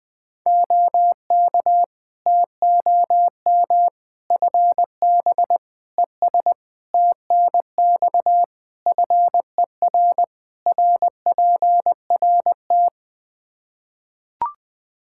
Morse code, heard as a keyed tone, about 20 wpm, 700 Hz. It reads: OK TOM FB ES TNX FER RPRT
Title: Practice QSO - Exchange 2 - Info - 20wpm